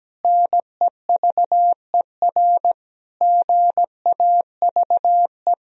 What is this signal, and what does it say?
Morse code, heard as a keyed tone, about 17 wpm, 700 Hz. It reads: NEVER GAVE
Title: Sentences from Top 1000 Words - Review Part 2 - 17wpm